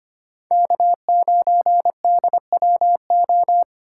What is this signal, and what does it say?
Morse code, heard as a keyed tone, about 33 wpm, 700 Hz. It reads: K9DWO